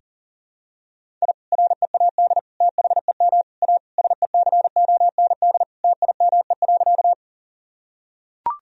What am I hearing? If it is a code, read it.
Morse code, heard as a keyed tone, about 40 wpm, 700 Hz. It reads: I READ THEM A SECOND TIME.
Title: Sentences from Top 1000 Words - Review Part 1 - 40wpm